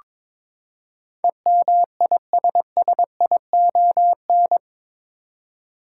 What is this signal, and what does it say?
Morse code, heard as a keyed tone, about 21 wpm, 700 Hz. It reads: EMISSION
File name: practice / 22wpm / Eight Letter Words - 22wpm